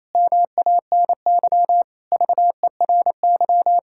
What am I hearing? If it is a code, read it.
Morse code, heard as a keyed tone, about 28 wpm, 700 Hz. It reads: MANY VERY
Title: Pangram Challenge v2 - 28wpm